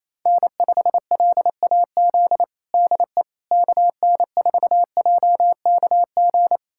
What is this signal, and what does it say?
Morse code, heard as a keyed tone, about 28 wpm, 700 Hz. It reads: N5LAZ DE KN4JKG
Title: USA Callsign DE Callsign 28wpm